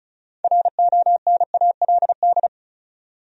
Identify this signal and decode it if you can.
Morse code, heard as a keyed tone, about 36 wpm, 700 Hz. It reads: RONALD
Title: USA Common Names - 35wpm